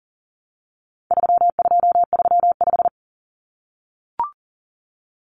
Morse code, heard as a keyed tone, about 39 wpm, 700 Hz. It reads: 3235
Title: Numbers - 4-Digit 40wpm